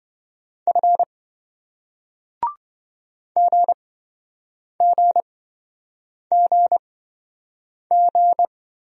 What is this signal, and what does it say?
Morse code, heard as a keyed tone, about 28 wpm, spaced as for 7 wpm, 700 Hz. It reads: F GGGG